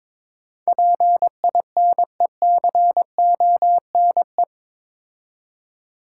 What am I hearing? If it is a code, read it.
Morse code, heard as a keyed tone, about 21 wpm, 700 Hz. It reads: PINECONE
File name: practice / 22wpm / Difficult Words with Hints - 22wpm